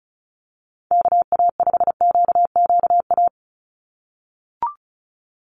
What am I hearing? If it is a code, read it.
Morse code, heard as a keyed tone, about 34 wpm, 700 Hz. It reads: KA5QQA
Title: NA SOTA Chaser Callsigns 35wpm